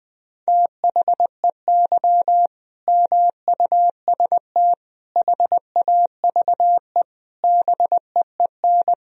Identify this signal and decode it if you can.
Morse code, heard as a keyed tone, about 21 wpm, 700 Hz. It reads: THEY MUST HAVE BEEN
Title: Sentences from Top 1000 Words - Review Part 2 - 20wpm